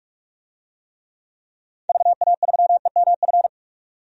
Morse code, heard as a keyed tone, about 46 wpm, 700 Hz. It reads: VA3EMF
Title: DXCC-100 Call Signs - 6 Characters - 45wpm